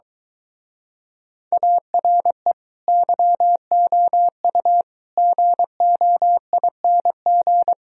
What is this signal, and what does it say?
Morse code, heard as a keyed tone, about 23 wpm, 700 Hz. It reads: ARE YOU GOING